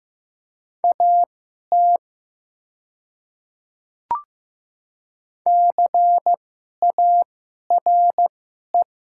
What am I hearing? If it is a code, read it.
Morse code, heard as a keyed tone, about 15 wpm, spaced as for 11 wpm, 700 Hz. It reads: AT CARE
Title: 2x Character Spacing - Top 300 Words - 15wpm